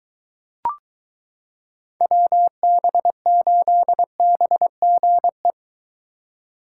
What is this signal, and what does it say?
Morse code, heard as a keyed tone, about 23 wpm, 700 Hz. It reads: WB8BGE